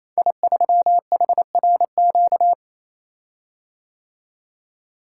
Morse code, heard as a keyed tone, about 29 wpm, 700 Hz. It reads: I3HRQ
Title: Code Groups - 5 Letter:Numbers - 28wpm